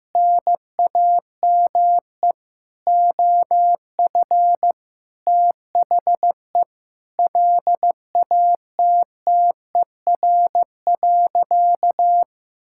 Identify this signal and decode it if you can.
Morse code, heard as a keyed tone, about 15 wpm, 700 Hz. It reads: NAME OF THE LATTER.
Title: Sentences from Top 1600 Words 15wpm